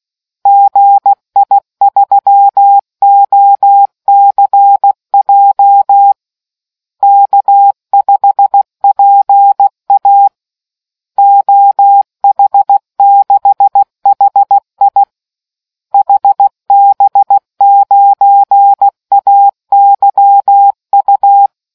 Morse code, heard as a keyed tone, about 16 wpm, 800 Hz. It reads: GI3OCJ K5PA OH6HI HB9AYU